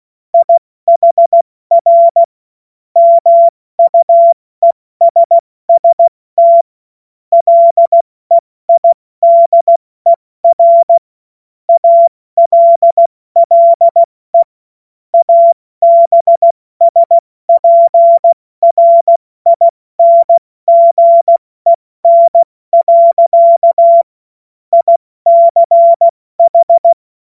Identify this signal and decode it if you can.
Morse code, heard as a keyed tone, about 16 wpm, 680 Hz. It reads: IHR MUESST LEIDER ALLE ABSPRINGEN. ICH
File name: Morsekurs / Witze / Witz 16. Februar 2021, 70 F80